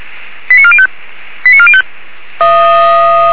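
Five-tone selective call (ZVEI), heard as quick runs of five tones, tones of about 70 ms. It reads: 89405 89405
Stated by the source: ZVEI five-tone selective call received with an SDR, archived by Feuerwehr Orlamünde.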